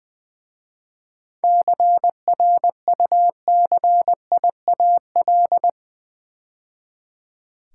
Morse code, heard as a keyed tone, about 20 wpm, 700 Hz. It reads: CRUCIAL
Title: 7 Letter Words - First 3 Letters Given - 20wpm